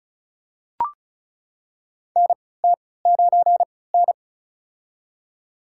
Morse code, heard as a keyed tone, about 34 wpm, spaced as for 20 wpm, 700 Hz. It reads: NT9N
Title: USA Call Signs - 3x Character Spacing - 35wpm